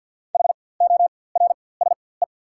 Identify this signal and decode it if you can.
Morse code, heard as a keyed tone, about 50 wpm, spaced as for 23 wpm, 690 Hz. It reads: HORSE
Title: 4x Character Spacing - Top 500 Words - 50wpm